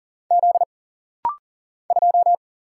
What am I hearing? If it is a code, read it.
Morse code, heard as a keyed tone, about 40 wpm, 700 Hz. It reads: Z 2
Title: Single Letter-Number - Rapid-Fire 40wpm